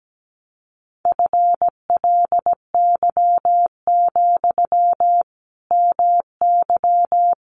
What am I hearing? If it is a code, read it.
Morse code, heard as a keyed tone, about 17 wpm, 700 Hz. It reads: FLY, MY
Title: Sentences from Top 500 Words 17wpm